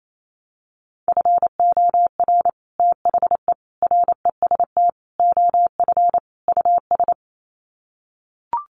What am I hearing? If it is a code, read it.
Morse code, heard as a keyed tone, about 28 wpm, 700 Hz. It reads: FOR THE REST OF US
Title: Sets of 5 Words Encore 28wpm